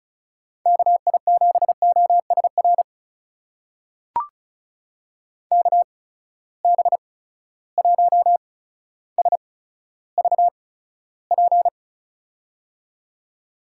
Morse code, heard as a keyed tone, about 36 wpm, spaced as for 15 wpm, 700 Hz. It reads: KI7OSR KB1SVP
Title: USA Call Signs - 6 Characters - 8x Character Spacing - 35wpm